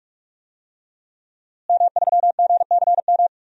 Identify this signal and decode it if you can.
Morse code, heard as a keyed tone, about 43 wpm, 700 Hz. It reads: M3GCM
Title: DXCC-100 Call Signs - 5 Characters - Repeated 2x - 45wpm